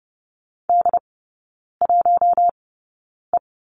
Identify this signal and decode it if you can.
Morse code, heard as a keyed tone, about 30 wpm, spaced as for 9 wpm, 700 Hz. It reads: D1E
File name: Lesson 17e - Letter u - Three Characters 30wpm